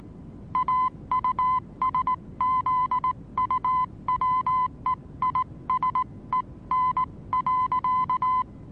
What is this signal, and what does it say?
Morse code, heard as a keyed tone, about 19 wpm, spaced as for 17 wpm, 1020 Hz. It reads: AUSZUWEISEN.
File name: repos / Morsekurs / DJ4UF Morsen26